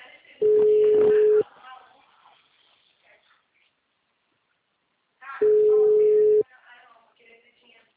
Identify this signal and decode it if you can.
Ringback tone (Europe, 425 Hz), with speech.